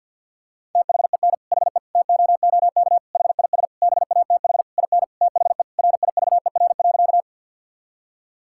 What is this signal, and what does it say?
Morse code, heard as a keyed tone, about 52 wpm, 700 Hz. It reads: THEN HE TOOK HIS BATH IN THE RIVER.
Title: Sentences from Top 1600 Words 50wpm